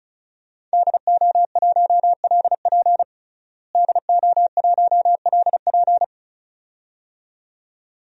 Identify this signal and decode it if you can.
Morse code, heard as a keyed tone, about 36 wpm, 700 Hz. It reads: DO1LP DO1LP
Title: DXCC-100 Call Signs - 5 Characters - Repeated 2x - 35wpm